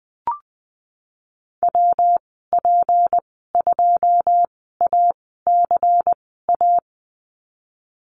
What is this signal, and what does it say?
Morse code, heard as a keyed tone, about 20 wpm, spaced as for 15 wpm, 700 Hz. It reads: WP2ACA
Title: 161 Lesson 26g - Letter g - Call Signs 20wpm